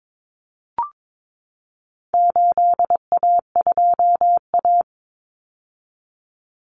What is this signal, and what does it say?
Morse code, heard as a keyed tone, about 22 wpm, 700 Hz. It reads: 8A2A
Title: Common Cut Numbers - 4-Digit 22wpm